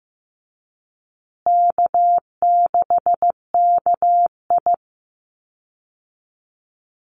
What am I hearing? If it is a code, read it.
Morse code, heard as a keyed tone, about 15 wpm, 700 Hz. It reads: K6KI
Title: Easy Call Signs Repeated 3x - 15wpm